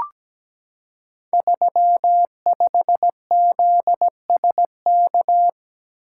Morse code, heard as a keyed tone, about 17 wpm, 700 Hz. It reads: 35ZSK
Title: Code Groups - 5 Letter:Numbers - 17wpm